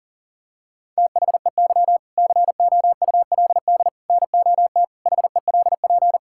Morse code, heard as a keyed tone, about 39 wpm, 700 Hz. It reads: THEY COULD NOT HELP
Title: Sentences from Top 1000 Words - Review Part 5 - 40wpm